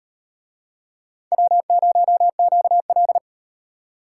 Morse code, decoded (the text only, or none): W0QL